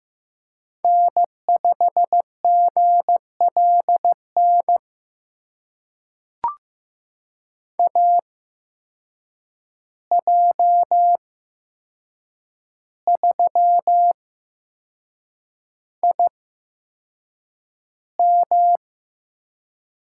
Morse code, heard as a keyed tone, about 15 wpm, spaced as for 6 wpm, 700 Hz. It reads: N5GLN AJ3IM
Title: USA Call Signs - 5 Characters - 8x Character Spacing - 15wpm